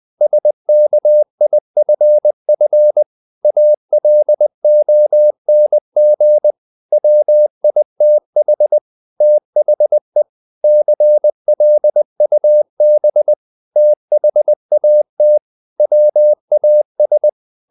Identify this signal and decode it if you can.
Morse code, heard as a keyed tone, about 20 wpm, 600 Hz. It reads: SKIFF ALONG WITH THE CLUB THAT WAS
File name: the old man and the sea / the old man and the sea part 1 20wpm